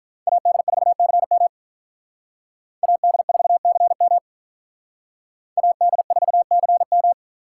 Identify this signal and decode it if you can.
Morse code, heard as a keyed tone, about 45 wpm, 700 Hz. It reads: AD4CM AD4CM AD4CM